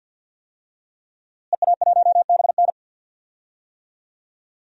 Morse code, heard as a keyed tone, about 48 wpm, 700 Hz. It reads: EA1BN
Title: DXCC-100 Call Signs - 5 Characters - Repeated 2x - 50wpm